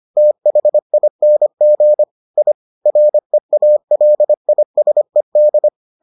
Morse code, heard as a keyed tone, about 25 wpm, 600 Hz. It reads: THING I REALISED